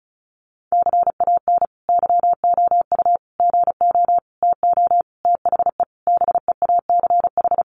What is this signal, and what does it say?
Morse code, heard as a keyed tone, about 34 wpm, 700 Hz. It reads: CAN YOU GO TO THE BEACH
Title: Sentences from Top 1500 Words 35wpm